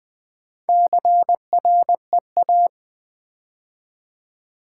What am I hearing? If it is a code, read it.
Morse code, heard as a keyed tone, about 20 wpm, 700 Hz. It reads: CREA
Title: Build A Word - 2 Letters At A Time - 8-Letter Words - 20wpm